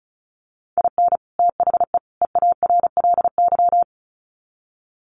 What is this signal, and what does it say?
Morse code, heard as a keyed tone, about 34 wpm, 700 Hz. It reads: IN THE EARLY
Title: Sets of 3 Words 35wpm